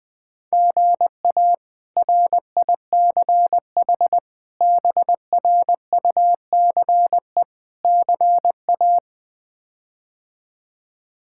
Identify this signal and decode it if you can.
Morse code, heard as a keyed tone, about 19 wpm, 700 Hz. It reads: GA RICH BRUCE CA
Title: K1USN SST QSO Practice - 20wpm